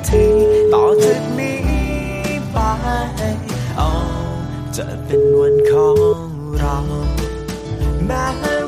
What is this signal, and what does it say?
Ringback tone (Europe, 425 Hz), with noise.